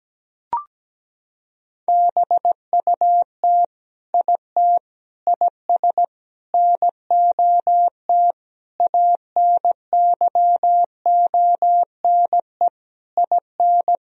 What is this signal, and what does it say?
Morse code, heard as a keyed tone, about 17 wpm, 700 Hz. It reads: BUT IT IS NOT ANYONE IN
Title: Sentences from Top 1000 Words - Review Part 4 - 17wpm